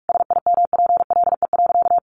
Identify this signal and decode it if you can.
Morse code, heard as a keyed tone, about 44 wpm, 700 Hz. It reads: SIMPLE.
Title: Sentences from Top 500 Words 45wpm